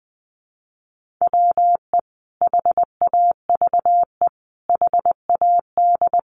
Morse code, heard as a keyed tone, about 19 wpm, 700 Hz. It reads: WE HAVE HAD
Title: Sentences from Top 1500 Words 20wpm